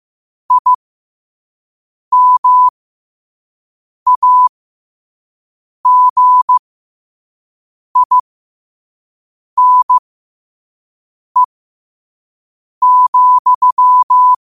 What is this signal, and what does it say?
Morse code, heard as a keyed tone, about 15 wpm, spaced as for 5 wpm, 1000 Hz. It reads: IMAGINE,
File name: qotdfiles / 5wpm